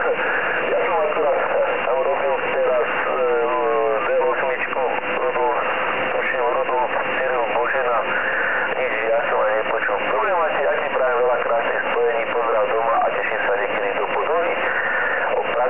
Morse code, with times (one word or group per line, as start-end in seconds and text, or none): none